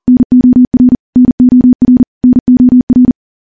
Morse code, heard as a keyed tone, about 43 wpm, 250 Hz. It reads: NOR NOR NOR
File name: morse / morse-training-40WPM